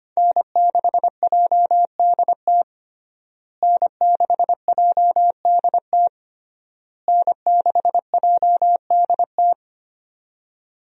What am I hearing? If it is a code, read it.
Morse code, heard as a keyed tone, about 25 wpm, 700 Hz. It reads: N6JDT N6JDT N6JDT
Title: USA Call Signs - 5 Characters - Repeated 3x - 25wpm